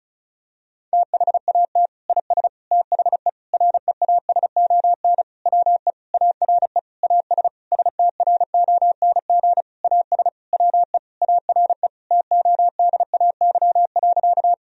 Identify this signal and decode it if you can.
Morse code, heard as a keyed tone, about 34 wpm, 700 Hz. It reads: THAT IS THE REASON WE ARE AS STRONG AS WE ARE TODAY.
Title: Sentences from Top 1000 Words - Review Part 4 - 35wpm